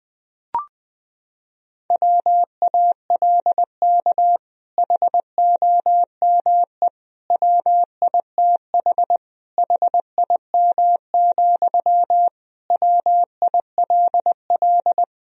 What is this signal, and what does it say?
Morse code, heard as a keyed tone, about 20 wpm, 700 Hz. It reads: WALK HOME WITH HIM, WILL
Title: Sentences from Top 1000 Words - Review Part 1 - 20wpm